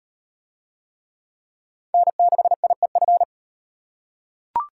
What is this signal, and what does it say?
Morse code, heard as a keyed tone, about 38 wpm, 700 Hz. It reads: N6IEF